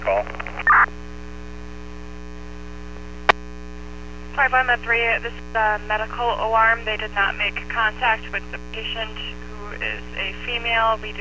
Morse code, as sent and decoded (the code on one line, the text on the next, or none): none